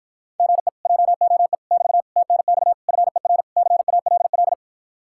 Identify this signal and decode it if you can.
Morse code, heard as a keyed tone, about 54 wpm, 700 Hz. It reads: GE JOE <BT> TNX FER CALL